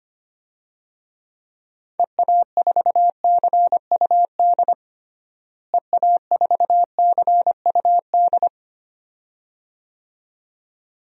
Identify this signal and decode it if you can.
Morse code, heard as a keyed tone, about 25 wpm, 700 Hz. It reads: EA4CUD EA4CUD
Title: DXCC-100 Call Signs - 6 Characters - Repeated 2x - 25wpm